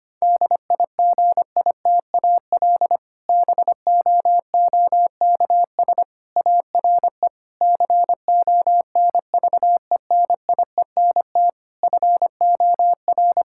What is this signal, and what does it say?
Morse code, heard as a keyed tone, about 25 wpm, 700 Hz. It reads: DIGITAL BOOKS ARE CONVENIENT FOR
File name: practice / 25wpm / Difficult Sentences with Hints - 25wpm